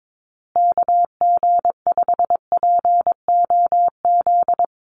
Morse code, heard as a keyed tone, about 22 wpm, 700 Hz. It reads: KG5POZ
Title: USA Call Signs Repeated 2x 22wpm